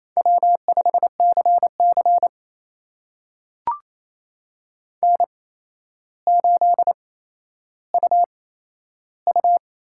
Morse code, heard as a keyed tone, about 28 wpm, spaced as for 12 wpm, 700 Hz. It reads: W5CC N8UU
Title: USA Call Signs - 8x Character Spacing - 28wpm